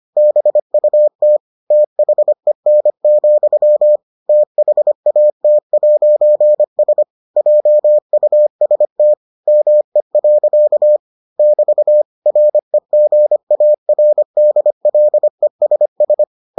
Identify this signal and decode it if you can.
Morse code, heard as a keyed tone, about 25 wpm, 600 Hz. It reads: BUT THEN, THAT'S JUST ME. <BT> REGARDLESS